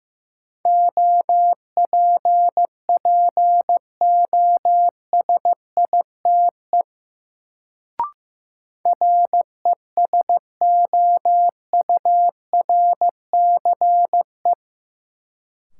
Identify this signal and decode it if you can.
Morse code, heard as a keyed tone, about 15 wpm, 700 Hz. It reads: OPPOSITE RESOURCE
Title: Sentences from Top 1300 Words - New Word Drill - 15wpm